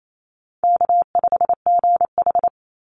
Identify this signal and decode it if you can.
Morse code, heard as a keyed tone, about 28 wpm, 700 Hz. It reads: K5GH
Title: Easy Call Signs - 28wpm